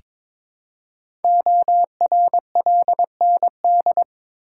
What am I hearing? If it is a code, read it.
Morse code, heard as a keyed tone, about 22 wpm, 700 Hz. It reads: ORLND
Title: Code Groups - 5 Letters - 22wpm